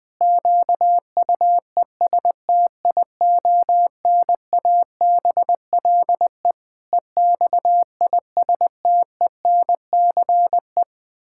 Morse code, heard as a keyed tone, about 20 wpm, 700 Hz. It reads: QUESTIONABLE EXISTENCE
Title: Pangram Challenge v2 - 20wpm